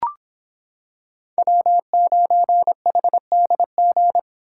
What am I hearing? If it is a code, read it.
Morse code, heard as a keyed tone, about 26 wpm, 700 Hz. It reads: W9HDG